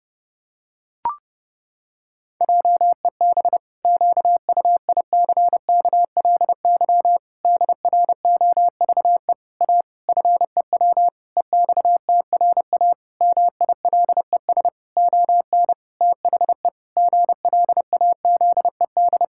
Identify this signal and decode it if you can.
Morse code, heard as a keyed tone, about 30 wpm, 700 Hz. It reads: JEB QUICKLY DROVE A FEW EXTRA MILES ON THE GLAZED
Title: Pangram Challenge v2 - 30wpm